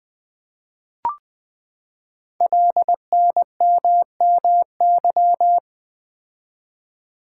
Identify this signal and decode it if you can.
Morse code, heard as a keyed tone, about 20 wpm, 700 Hz. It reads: LNMMY
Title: Code Groups - 5 Letters - 20wpm